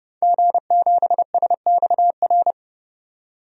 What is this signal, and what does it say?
Morse code, heard as a keyed tone, about 30 wpm, 700 Hz. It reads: G7SXR
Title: DXCC-100 Call Signs - 5 Characters - 30wpm